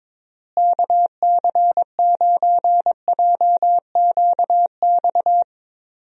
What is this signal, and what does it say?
Morse code, heard as a keyed tone, about 22 wpm, 700 Hz. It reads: KC9JQX
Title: USA Call Signs - 6 Characters - Repeated 3x - 22wpm